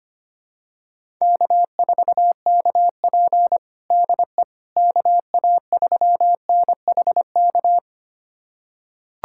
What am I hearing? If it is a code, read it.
Morse code, heard as a keyed tone, about 25 wpm, 700 Hz. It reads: K4KP DE KA3NHK
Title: USA Callsign DE Callsign 25wpm